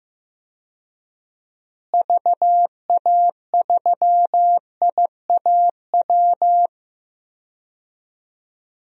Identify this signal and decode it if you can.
Morse code, heard as a keyed tone, about 15 wpm, 700 Hz. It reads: VA3IAW
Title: DXCC-100 Call Signs - 6 Characters - Repeated 3x - 15wpm